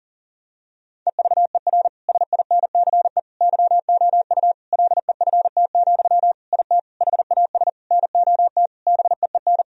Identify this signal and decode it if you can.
Morse code, heard as a keyed tone, about 37 wpm, 700 Hz. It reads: EVER SINCE YOU LEFT, IT HAS NOT BEEN
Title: Sentences from Top 1000 Words - Review Part 1 - 40wpm